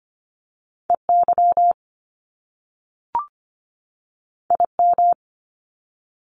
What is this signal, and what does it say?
Morse code, heard as a keyed tone, about 25 wpm, 700 Hz. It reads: EY IM
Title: Top Two Letter Combinations in Words 25wpm